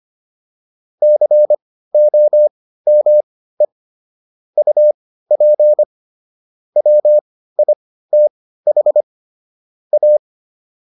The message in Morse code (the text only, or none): COME UP WITH A